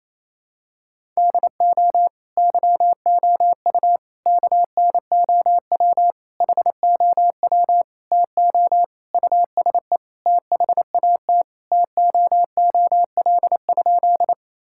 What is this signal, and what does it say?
Morse code, heard as a keyed tone, about 28 wpm, 700 Hz. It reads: DO YOU KNOW HOW TO USE THAT TOOL?
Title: Sentences from Top 1000 Words - Review Part 6 - 28wpm